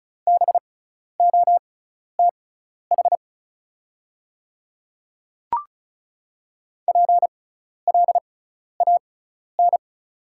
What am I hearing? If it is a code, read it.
Morse code, heard as a keyed tone, about 34 wpm, spaced as for 12 wpm, 690 Hz. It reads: BOTH PLAN
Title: 6x Character Spacing - Top 500 Words - 35wpm